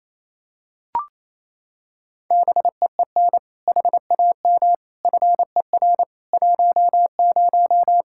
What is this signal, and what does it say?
Morse code, heard as a keyed tone, about 28 wpm, 700 Hz. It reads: BEEN HAM FER 10
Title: Practice QSO - Exchange 3 - More Info - 28wpm